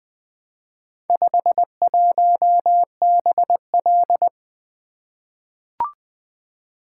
Morse code, heard as a keyed tone, about 20 wpm, 700 Hz. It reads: 51BL
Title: Four Letter-Number Combinations - 20wpm